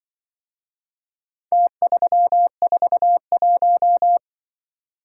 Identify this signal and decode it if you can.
Morse code, heard as a keyed tone, about 24 wpm, 700 Hz. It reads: T341
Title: Speed-Racing - Four Letter-Number Combinations - 20wpm